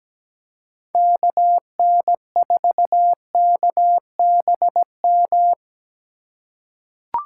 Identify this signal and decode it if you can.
Morse code, heard as a keyed tone, about 17 wpm, 700 Hz. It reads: KN4KBM
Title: USA Call Signs - 6 Characters - 4x Character Spacing - 17wpm